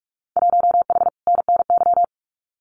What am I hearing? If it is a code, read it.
Morse code, heard as a keyed tone, about 44 wpm, 700 Hz. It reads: 1H NNY
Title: Winter Field Day Exchanges 45wpm